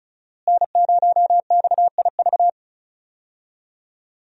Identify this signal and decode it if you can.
Morse code, heard as a keyed tone, about 34 wpm, 700 Hz. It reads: N0XIV
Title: USA Call Signs - 5 Characters - Repeated 3x - 35wpm